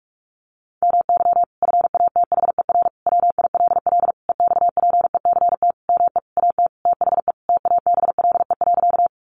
Morse code, heard as a keyed tone, about 44 wpm, 700 Hz. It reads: MY FATHER WILL EXPECT ME AT THE TABLE.
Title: Sentences from Top 1100 Words 45wpm